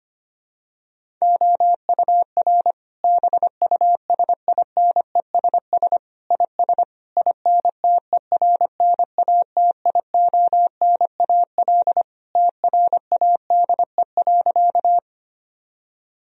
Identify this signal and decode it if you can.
Morse code, heard as a keyed tone, about 25 wpm, 700 Hz. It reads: OUR BUSINESS IS INTERNATIONAL TRADE.
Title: Sentences from Top 1000 Words - Review Part 4 - 25wpm